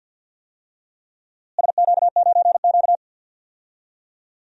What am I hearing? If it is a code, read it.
Morse code, heard as a keyed tone, about 46 wpm, 700 Hz. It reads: SQ9Q
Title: DXCC-100 Call Signs - 4 Characters - Repeated 3x - 50wpm